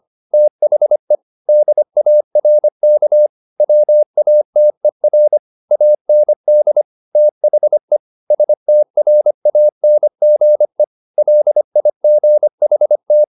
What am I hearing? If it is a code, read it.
Morse code, heard as a keyed tone, about 25 wpm, 600 Hz. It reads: THE DARK WATER AND THE STRANGE LIGHT
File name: the old man and the sea part 1 25wpm